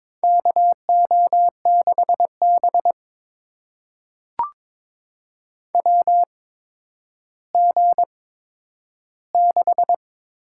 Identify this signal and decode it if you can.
Morse code, heard as a keyed tone, about 22 wpm, spaced as for 11 wpm, 700 Hz. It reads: KO6B WG6
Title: USA Call Signs - 4 Characters - 8x Character Spacing - 22wpm